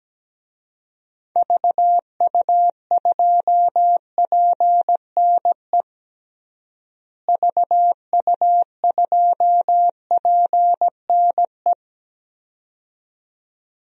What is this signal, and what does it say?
Morse code, heard as a keyed tone, about 17 wpm, 700 Hz. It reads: VU2PNE VU2PNE
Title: DXCC-100 Call Signs - 6 Characters - Repeated 2x - 17wpm